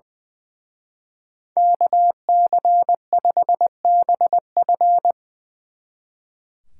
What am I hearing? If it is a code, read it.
Morse code, heard as a keyed tone, about 23 wpm, 700 Hz. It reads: KC5BF
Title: Speed Racing - USA 2x2 Call Signs 20wpm